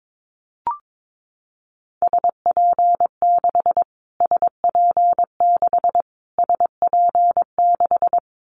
Morse code, heard as a keyed tone, about 22 wpm, 700 Hz. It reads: SP6 SP6 SP6
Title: Callsign Prefixes for Top 25 DXCC - Repeated 3x - 22wpm